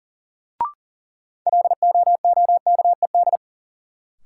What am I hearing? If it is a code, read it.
Morse code, heard as a keyed tone, about 40 wpm, 700 Hz. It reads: LOOKED
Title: Sentences from Top 800 Words - New Word Drill - 40wpm